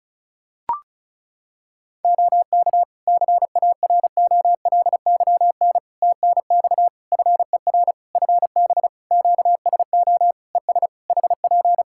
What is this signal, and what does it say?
Morse code, heard as a keyed tone, about 36 wpm, 700 Hz. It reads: OK CAROLYN TNX FER FB QSO ES HP
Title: Practice QSO - Exchange 4 - Ending - 35wpm